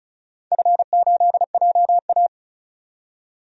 Morse code, heard as a keyed tone, about 34 wpm, 700 Hz. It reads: F8JA
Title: DXCC-100 Call Signs - 4 Characters - 35wpm